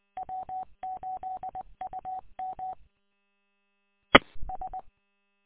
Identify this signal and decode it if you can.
Morse code, heard as a keyed tone, about 20 wpm, 750 Hz. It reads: W8UM ES
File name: w8um- -combined